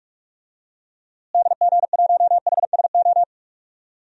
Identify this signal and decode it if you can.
Morse code, heard as a keyed tone, about 46 wpm, 700 Hz. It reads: DG1HSO
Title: DXCC-100 Call Signs - 6 Characters - Repeated 2x - 45wpm